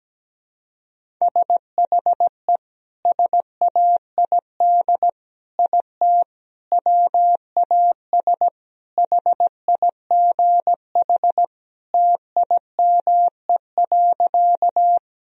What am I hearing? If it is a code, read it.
Morse code, heard as a keyed tone, about 17 wpm, 700 Hz. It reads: SHE SAID IT WAS HIGH TIME.
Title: Sentences from Top 1000 Words - Review Part 1 - 17wpm